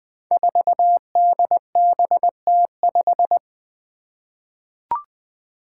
Morse code, heard as a keyed tone, about 20 wpm, 700 Hz. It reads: 4DBT5
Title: Five Challenging Letter-Number Combinations 20wpm